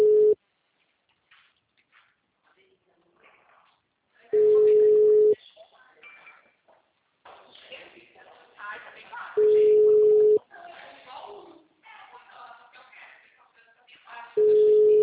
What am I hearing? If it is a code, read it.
Ringback tone (Europe, 425 Hz), with speech.